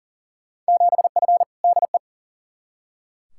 Morse code, heard as a keyed tone, about 40 wpm, 700 Hz. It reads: 7F DE
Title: Field Day Exchanges Encore - 40wpm